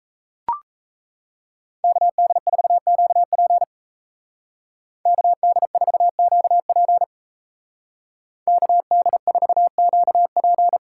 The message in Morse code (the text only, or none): KD4QP KD4QP KD4QP